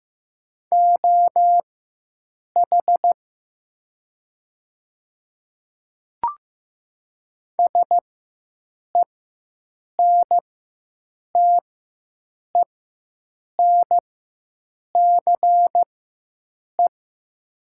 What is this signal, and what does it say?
Morse code, heard as a keyed tone, about 15 wpm, spaced as for 7 wpm, 700 Hz. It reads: OH SENTENCE
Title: 4x Character Spacing - Top 500 Words - 15wpm